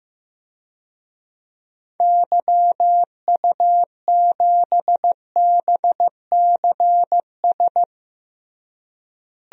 Morse code, heard as a keyed tone, about 15 wpm, 700 Hz. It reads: YU7BCS
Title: DXCC-100 Call Signs - 6 Characters - Repeated 2x - 15wpm